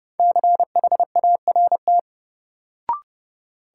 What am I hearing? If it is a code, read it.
Morse code, heard as a keyed tone, about 30 wpm, 700 Hz. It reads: CHART
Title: Sentences from Top 1300 Words - New Word Drill Alt - 30wpm